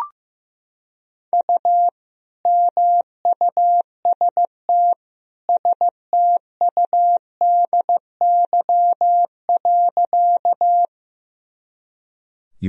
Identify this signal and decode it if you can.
Morse code, heard as a keyed tone, about 16 wpm, 700 Hz. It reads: U MUST STUDY.
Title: Sentences from Top 300 Words and Abbreviations 15wpm